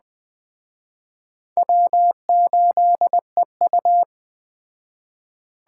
Morse code, heard as a keyed tone, about 20 wpm, 700 Hz. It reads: W8EU
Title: Speed Racing - USA Call Signs 17wpm